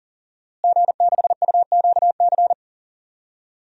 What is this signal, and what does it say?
Morse code, heard as a keyed tone, about 37 wpm, 700 Hz. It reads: G6UQC
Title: DXCC-100 Call Signs - 5 Characters - Repeated 2x - 40wpm